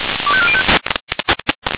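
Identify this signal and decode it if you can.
Five-tone selective call (ZVEI), heard as quick runs of five tones, tones of about 70 ms. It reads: 25405
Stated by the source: ZVEI five-tone selective call 25304 received by SDR.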